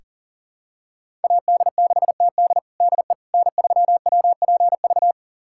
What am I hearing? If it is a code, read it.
Morse code, heard as a keyed tone, about 40 wpm, 700 Hz. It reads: AD6TD DE N3WPV
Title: USA Callsign DE Callsign - 5 Characters - 40wpm